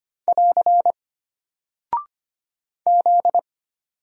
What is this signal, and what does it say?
Morse code, heard as a keyed tone, about 25 wpm, 700 Hz. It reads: <AR> Z
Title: Single Character (Letter:Number:Punctuation:Prosign) Alt 25wpm